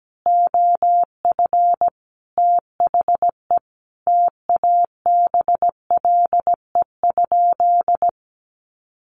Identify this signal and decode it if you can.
Morse code, heard as a keyed tone, about 17 wpm, 700 Hz. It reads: OF THE TABLE?
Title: Sentences from Top 1300 Words 17wpm